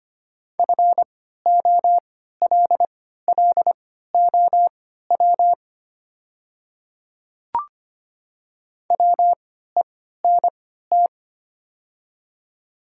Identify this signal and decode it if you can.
Morse code, heard as a keyed tone, about 25 wpm, spaced as for 14 wpm, 700 Hz. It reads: FOLLOW WENT
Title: 3x Character Spacing - Top 200 Words - 25wpm